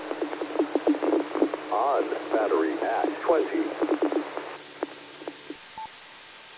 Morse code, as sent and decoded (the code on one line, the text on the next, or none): none